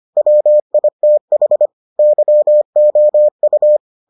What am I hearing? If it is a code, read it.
Morse code, heard as a keyed tone, about 25 wpm, 600 Hz. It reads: WITH YOU